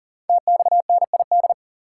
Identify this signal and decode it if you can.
Morse code, heard as a keyed tone, about 39 wpm, 700 Hz. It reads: TXNID
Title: Code Groups - 5 Letter:Numbers - 40wpm